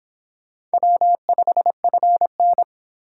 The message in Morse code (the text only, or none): W5FN